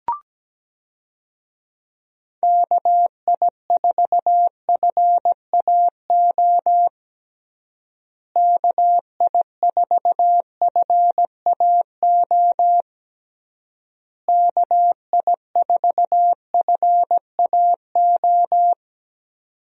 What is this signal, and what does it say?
Morse code, heard as a keyed tone, about 17 wpm, 700 Hz. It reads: KI4FAO KI4FAO KI4FAO
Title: USA Call Signs - 6 Characters - Repeated 3x - 17wpm